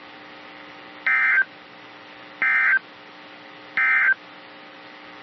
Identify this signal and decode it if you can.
Emergency Alert System (SAME) tones, heard as harsh buzzing bursts. It is heard from 0.9 to 4.2 s.
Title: NOAA Weather Radio - KEC55